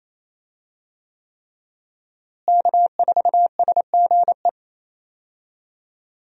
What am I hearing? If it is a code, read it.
Morse code, heard as a keyed tone, about 28 wpm, 700 Hz. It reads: K4SGE
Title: USA Call Signs - 5 Characters - Repeated 3x - 28wpm